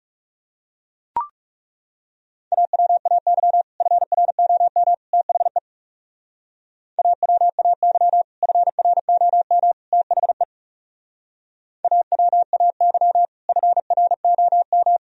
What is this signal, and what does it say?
Morse code, heard as a keyed tone, about 40 wpm, 700 Hz. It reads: AWAY FROM THE AWAY FROM THE AWAY FROM